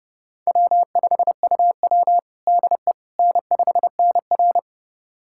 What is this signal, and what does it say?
Morse code, heard as a keyed tone, about 30 wpm, 700 Hz. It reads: W5UW DE N5NR
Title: USA Callsign DE Callsign - 4 Characters - 30wpm